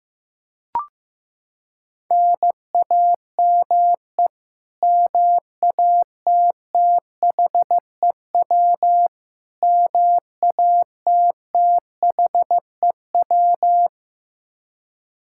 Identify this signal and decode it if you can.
Morse code, heard as a keyed tone, about 15 wpm, 700 Hz. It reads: NAME MATTHEW MATTHEW
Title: K1USN SST QSO Practice 15wpm